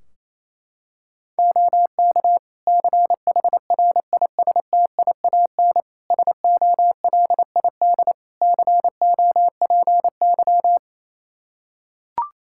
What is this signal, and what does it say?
Morse code, heard as a keyed tone, about 28 wpm, 700 Hz. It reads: OK CHRISTIAN SOLID COPY
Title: Practice QSO - Exchange 3 - More Info - 28wpm